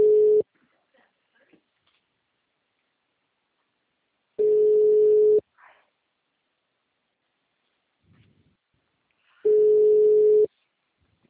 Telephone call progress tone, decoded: ringback tone

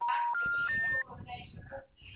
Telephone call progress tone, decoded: special information tone

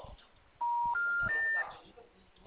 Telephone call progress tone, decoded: special information tone

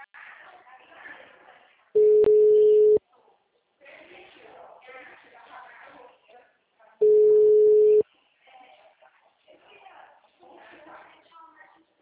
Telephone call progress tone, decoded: ringback tone

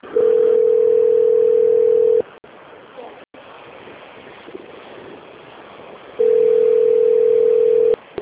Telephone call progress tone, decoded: ringback tone

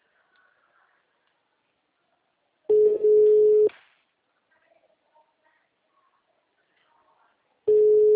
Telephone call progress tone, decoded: ringback tone

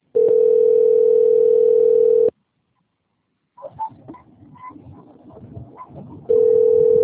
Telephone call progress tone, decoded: ringback tone

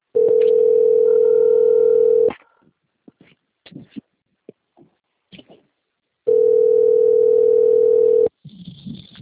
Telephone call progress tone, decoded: ringback tone

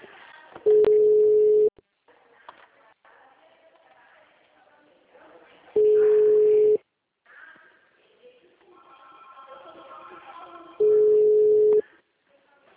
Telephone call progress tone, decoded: ringback tone

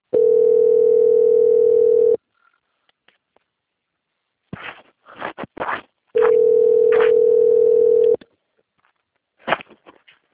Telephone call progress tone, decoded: ringback tone

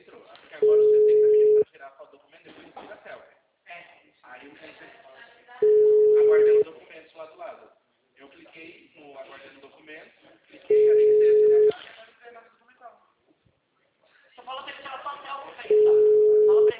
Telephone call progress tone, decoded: ringback tone

